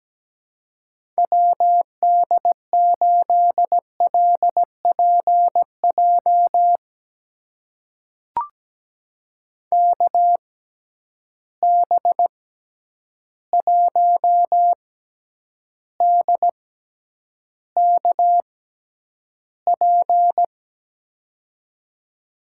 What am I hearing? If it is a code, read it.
Morse code, heard as a keyed tone, about 17 wpm, spaced as for 9 wpm, 700 Hz. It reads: WD8LPJ KB1DKP
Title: USA Call Signs - 6 Characters - 6x Character Spacing - 17wpm